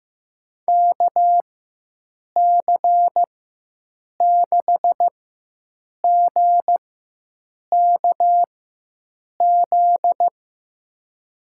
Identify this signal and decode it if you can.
Morse code, heard as a keyed tone, about 15 wpm, spaced as for 7 wpm, 700 Hz. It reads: KC6GKZ
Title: USA Call Signs - 4x Character Spacing - 15wpm